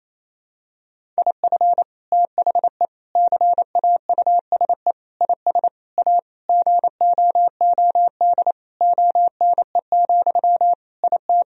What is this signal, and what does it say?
Morse code, heard as a keyed tone, about 28 wpm, 700 Hz. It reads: IF THE CAUSE IS A GOOD ONE, IT
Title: Sentences from Top 1000 Words - Review Part 1 - 28wpm